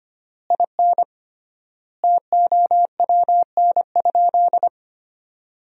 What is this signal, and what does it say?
Morse code, heard as a keyed tone, about 25 wpm, 700 Hz. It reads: IN TOWN?
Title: 3x Word-Spacing - Sentences from Top 500 Words - 25wpm